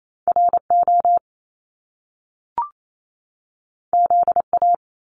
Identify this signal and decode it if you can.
Morse code, heard as a keyed tone, about 28 wpm, 700 Hz. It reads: RO ZA
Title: Top Two Letter Combinations in Words 28wpm